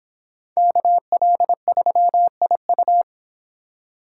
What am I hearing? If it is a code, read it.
Morse code, heard as a keyed tone, about 30 wpm, 700 Hz. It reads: KL3IU